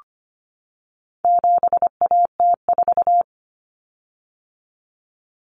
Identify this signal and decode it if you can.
Morse code, heard as a keyed tone, about 25 wpm, 700 Hz. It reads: 7AT4
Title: Common Cut Numbers - 4-Digit 25wpm